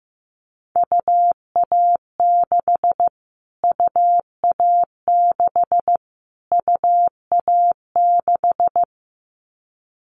Morse code, heard as a keyed tone, about 15 wpm, 700 Hz. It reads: UA6 UA6 UA6
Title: Callsign Prefixes for Top 25 DXCC - Repeated 3x - 15wpm